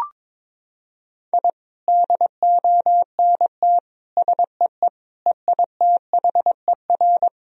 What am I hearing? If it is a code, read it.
Morse code, heard as a keyed tone, about 21 wpm, 700 Hz. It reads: I DONT SEE EITHER
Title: Sentences from Top 1000 Words - Review Part 4 - 22wpm